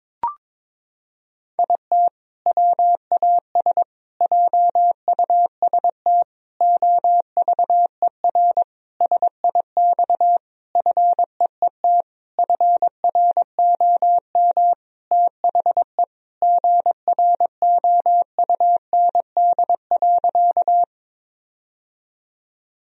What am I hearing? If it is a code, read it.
Morse code, heard as a keyed tone, about 22 wpm, 700 Hz. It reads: IT WAS JUST OVER SIX FEET FROM THE GROUND.
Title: Sentences from Top 1000 Words - Review Part 2 - 22wpm